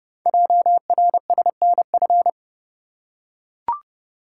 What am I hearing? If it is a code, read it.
Morse code, heard as a keyed tone, about 30 wpm, 700 Hz. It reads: JRSNF
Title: Code Groups - 5 Letters - 30wpm